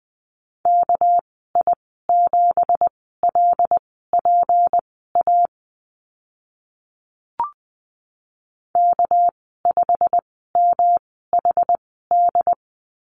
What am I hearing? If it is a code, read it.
Morse code, heard as a keyed tone, about 20 wpm, spaced as for 15 wpm, 700 Hz. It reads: KI7LPA K5MHD
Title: 200 Lesson 32g - Letter k - Call Signs 20wpm